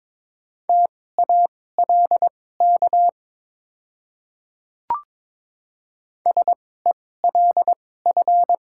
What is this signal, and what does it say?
Morse code, heard as a keyed tone, about 22 wpm, spaced as for 16 wpm, 700 Hz. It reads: TALK SELF
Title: 2x Character Spacing - Top 500 Words - 22wpm